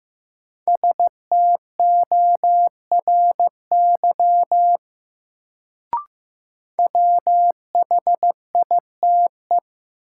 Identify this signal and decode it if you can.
Morse code, heard as a keyed tone, about 15 wpm, 700 Hz. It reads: STORY WHITE
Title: Sentences from Top 300 Words - New Word Drill - 15wpm